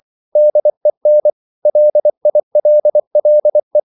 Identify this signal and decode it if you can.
Morse code, heard as a keyed tone, about 24 wpm, 600 Hz. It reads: DEN LILLE
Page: Ferie 24 wpm